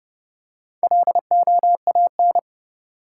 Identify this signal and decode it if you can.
Morse code, heard as a keyed tone, about 31 wpm, 700 Hz. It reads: LOAN
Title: Four Letter Words - 30wpm